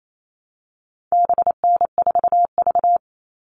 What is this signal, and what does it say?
Morse code, heard as a keyed tone, about 28 wpm, 700 Hz. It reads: BN4V
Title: All Cut Numbers - 4-Digit 28wpm